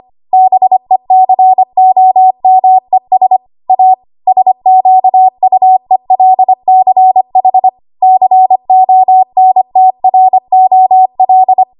Morse code, heard as a keyed tone, about 25 wpm, 750 Hz. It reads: BECOMES A SQUELCH CONTROL
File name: Archive / 25 WPM / 25WPM